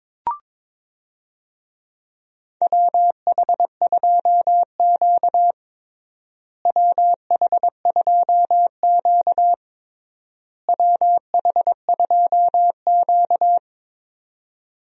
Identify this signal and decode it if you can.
Morse code, heard as a keyed tone, about 22 wpm, 700 Hz. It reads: WH2Q WH2Q WH2Q
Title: USA Call Signs - 4 Characters - Repeated 3x - 22wpm